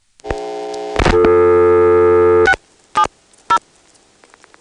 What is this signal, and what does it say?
DTMF, heard as telephone touch tones, tones of about 80 ms. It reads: 670
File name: phonetrips / PrinceWmCoVA5HQ